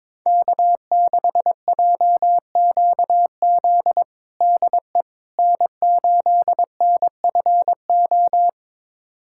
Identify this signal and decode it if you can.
Morse code, heard as a keyed tone, about 22 wpm, 700 Hz. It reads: K6JQZ DE N8NFO
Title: USA Callsign DE Callsign - 5 Characters - 22wpm